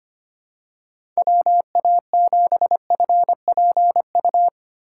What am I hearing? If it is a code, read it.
Morse code, heard as a keyed tone, about 25 wpm, 700 Hz. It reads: WA7FPU
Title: USA Call Signs - 6 Characters - 4x Character Spacing - 25wpm